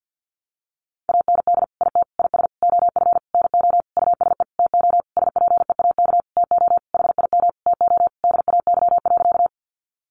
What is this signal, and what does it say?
Morse code, heard as a keyed tone, about 50 wpm, 700 Hz. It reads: AND IT IS OF NO USE TO SPEAK TO HIM TO DAY.
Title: Sentences from Top 1100 Words 50wpm